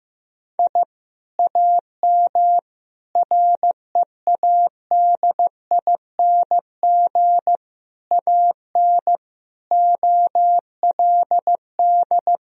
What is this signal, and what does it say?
Morse code, heard as a keyed tone, about 14 wpm, 700 Hz. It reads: I AM READING AN OLD
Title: Sentences from Top 1000 Words - Review Part 4 - 15wpm